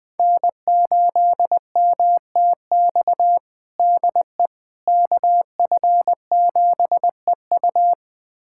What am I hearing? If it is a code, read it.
Morse code, heard as a keyed tone, about 21 wpm, 700 Hz. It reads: N8MTX DE KF7EU
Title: USA Callsign DE Callsign - 5 Characters - 20wpm